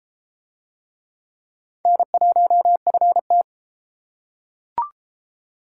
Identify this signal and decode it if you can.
Morse code, heard as a keyed tone, about 33 wpm, 700 Hz. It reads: N1FT